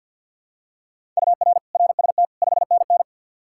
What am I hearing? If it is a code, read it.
Morse code, heard as a keyed tone, about 48 wpm, 700 Hz. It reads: UR RST 5NN
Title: Practice QSO - Exchange 1 - Beginning - 50wpm